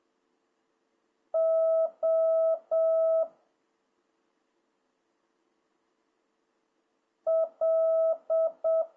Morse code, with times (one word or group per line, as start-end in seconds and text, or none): 1.3-3.2 O
7.3-8.8 L